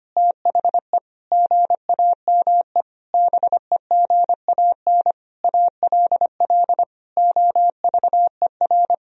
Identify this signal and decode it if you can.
Morse code, heard as a keyed tone, about 25 wpm, 700 Hz. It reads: THE GAME BEGAN ALL OVER
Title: Sentences from Top 500 Words and Abbreviations 25wpm